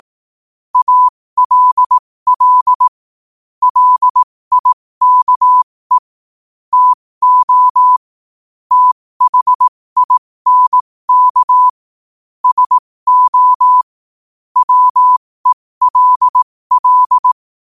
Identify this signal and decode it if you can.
Morse code, heard as a keyed tone, about 18 wpm, spaced as for 15 wpm, 1000 Hz. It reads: ALL LIKE TO THINK SO WELL
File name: qotdfiles / 15wpm